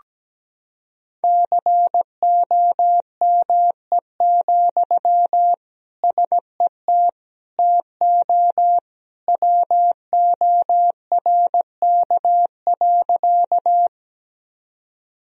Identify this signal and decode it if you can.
Morse code, heard as a keyed tone, about 17 wpm, 700 Hz. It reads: COME, SET TO WORK.
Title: Sentences from Top 1000 Words - Review Part 1 - 17wpm